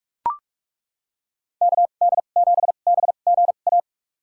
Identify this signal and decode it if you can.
Morse code, heard as a keyed tone, about 41 wpm, spaced as for 31 wpm, 700 Hz. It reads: KD7BGA
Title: USA Call Signs - 6 Characters - 2x Character Spacing - 45wpm